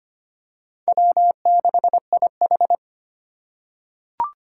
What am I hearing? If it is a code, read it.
Morse code, heard as a keyed tone, about 25 wpm, 700 Hz. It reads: W6IH
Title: USA Call Signs - 4 Characters - 3x Character Spacing - 25wpm